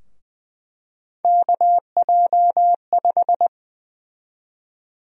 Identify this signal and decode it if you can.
Morse code, heard as a keyed tone, about 20 wpm, 700 Hz. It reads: KJ5
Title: USA Call Sign Prefixes - Repeated 2x - 20wpm